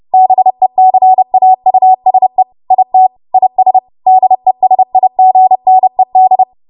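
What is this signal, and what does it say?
Morse code, heard as a keyed tone, about 30 wpm, 750 Hz. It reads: BECAUSE IT IS DESIGNED
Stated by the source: W1AW Web Code Practice Files, 30 WPM run from ARRL; text from April 2023 QST.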